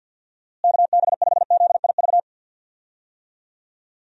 Morse code, heard as a keyed tone, about 53 wpm, 700 Hz. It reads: KB5ZIV
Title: USA Call Signs - 6 Characters - Repeated 3x - 50wpm